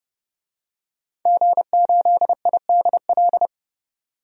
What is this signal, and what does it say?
Morse code, heard as a keyed tone, about 30 wpm, 700 Hz. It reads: G8IDL
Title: DXCC-100 Call Signs - 5 Characters - 30wpm